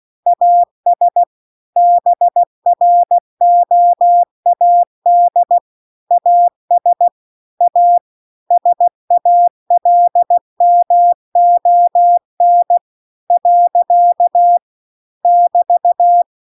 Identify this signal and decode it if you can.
Morse code, heard as a keyed tone, about 16 wpm, 700 Hz. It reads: AS BROAD AS A SALMON . <BT>